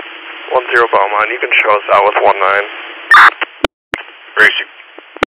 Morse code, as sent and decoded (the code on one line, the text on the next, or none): none